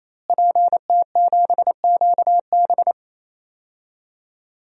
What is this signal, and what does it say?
Morse code, heard as a keyed tone, about 27 wpm, 700 Hz. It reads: PT7QB
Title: DXCC-100 Call Signs - 5 Characters - Repeated 3x - 28wpm